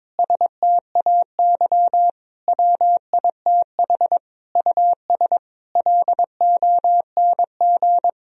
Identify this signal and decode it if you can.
Morse code, heard as a keyed tone, about 22 wpm, 700 Hz. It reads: STAY WITH US LONG
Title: Sentences from Top 1000 Words - Review Part 2 - 22wpm